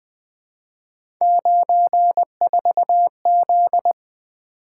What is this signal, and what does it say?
Morse code, heard as a keyed tone, about 21 wpm, 700 Hz. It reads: 94Z
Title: Three Letter-Number Combinations - 20wpm